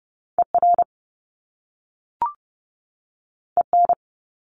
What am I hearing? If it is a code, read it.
Morse code, heard as a keyed tone, about 30 wpm, 700 Hz. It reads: ER EN
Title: Top Two Letter Combinations in Words 30wpm